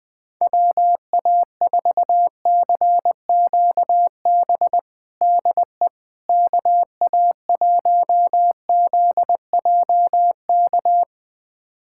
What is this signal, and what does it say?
Morse code, heard as a keyed tone, about 20 wpm, 700 Hz. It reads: WA4CQB DE KA1ZJK
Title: USA Callsign DE Callsign - 6 Characters - 20wpm